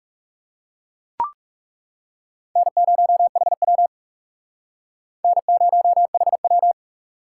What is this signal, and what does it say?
Morse code, heard as a keyed tone, about 38 wpm, 700 Hz. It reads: N0HW N0HW